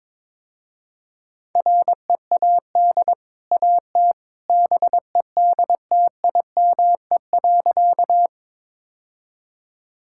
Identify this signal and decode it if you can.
Morse code, heard as a keyed tone, about 22 wpm, 690 Hz. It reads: READ AT BEDTIME.
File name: practice / 22wpm / Easy Sentences with Hints - 22wpm